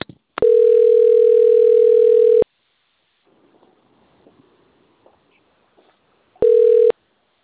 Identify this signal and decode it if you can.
Ringback tone (North America), 440 + 480 Hz.